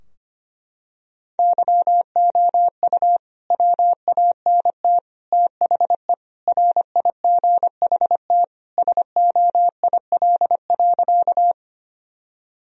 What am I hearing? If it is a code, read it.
Morse code, heard as a keyed tone, about 25 wpm, 700 Hz. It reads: YOU WANT THE RIGHT SOIL.
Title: Sentences from Top 1000 Words - Review Part 4 - 25wpm